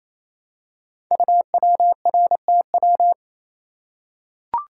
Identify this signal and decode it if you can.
Morse code, heard as a keyed tone, about 27 wpm, 700 Hz. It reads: UWRTW
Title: Code Groups - 5 Letters - 28wpm